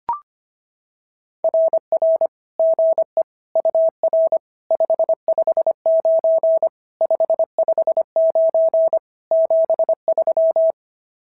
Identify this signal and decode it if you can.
Morse code, heard as a keyed tone, about 25 wpm, 660 Hz. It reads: RR GE UR 559 559 73
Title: SOTA QSO Practice 25wpm